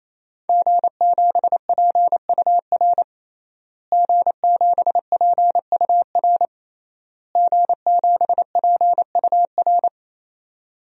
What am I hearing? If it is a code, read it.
Morse code, heard as a keyed tone, about 28 wpm, 700 Hz. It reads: G7PUR G7PUR G7PUR
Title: DXCC-100 Call Signs - 5 Characters - Repeated 3x - 28wpm